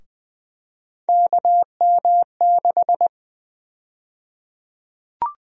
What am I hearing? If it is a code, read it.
Morse code, heard as a keyed tone, about 20 wpm, 700 Hz. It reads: KM6
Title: USA Call Sign Prefixes - 20wpm